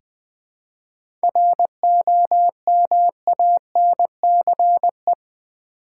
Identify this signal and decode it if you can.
Morse code, heard as a keyed tone, about 20 wpm, 700 Hz. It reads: ROMANCE
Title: 7 Letter Words - First 3 Letters Given - 20wpm